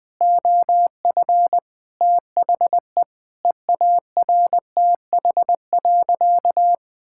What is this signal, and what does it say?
Morse code, heard as a keyed tone, about 20 wpm, 700 Hz. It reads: OF THE EARTH.
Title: Sentences from Top 1000 Words - Review Part 5 - 20wpm